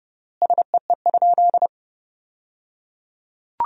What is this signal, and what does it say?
Morse code, heard as a keyed tone, about 30 wpm, 700 Hz. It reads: SEE?
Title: 3x Word-Spacing - Sentences from Top 100 Words - 30wpm